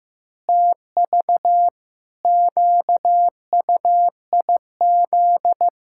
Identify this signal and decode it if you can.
Morse code, heard as a keyed tone, about 16 wpm, 700 Hz. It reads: TV QUIZ
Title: Pangram Challenge v2 - 15wpm